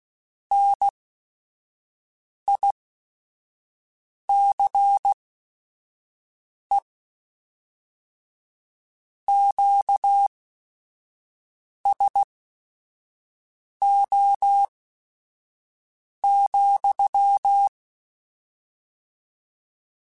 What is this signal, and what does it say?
Morse code, heard as a keyed tone, about 16 wpm, spaced as for 5 wpm, 800 Hz. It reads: NICE QSO,